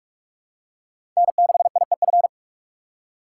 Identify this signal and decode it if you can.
Morse code, heard as a keyed tone, about 44 wpm, 700 Hz. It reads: N6IEF